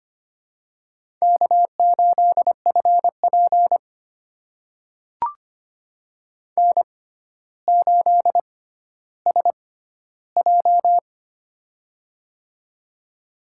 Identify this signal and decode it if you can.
Morse code, heard as a keyed tone, about 25 wpm, spaced as for 13 wpm, 700 Hz. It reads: K8FP N8SJ
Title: USA Call Signs - 4 Characters - 6x Character Spacing - 25wpm